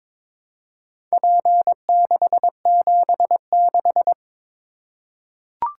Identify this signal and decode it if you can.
Morse code, heard as a keyed tone, about 22 wpm, 700 Hz. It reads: P676
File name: practice / 22wpm / Four Letter-Number Combinations - 22wpm